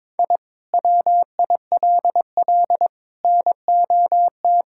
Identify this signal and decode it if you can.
Morse code, heard as a keyed tone, about 22 wpm, 700 Hz. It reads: I WILL NOT
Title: Sentences from Top 1000 Words - Review Part 6 - 22wpm